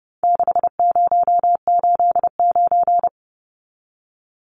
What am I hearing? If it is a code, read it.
Morse code, heard as a keyed tone, about 30 wpm, 700 Hz. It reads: 6089
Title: Numbers - 4-Digit 30wpm